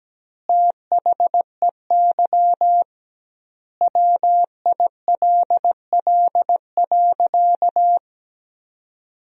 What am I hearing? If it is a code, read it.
Morse code, heard as a keyed tone, about 17 wpm, 700 Hz. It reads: THEY WILL.
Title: 2x Word-Spacing - Sentences from Top 500 Words - 17wpm